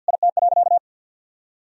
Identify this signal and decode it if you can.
Morse code, heard as a keyed tone, about 52 wpm, 700 Hz. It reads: IT.
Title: Sentences from Top 1000 Words - Review Part 1 - 50wpm